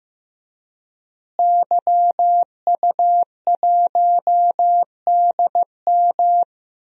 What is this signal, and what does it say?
Morse code, heard as a keyed tone, about 15 wpm, 700 Hz. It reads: YU1DM
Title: DXCC-100 Call Signs - 5 Characters - Repeated 3x - 15wpm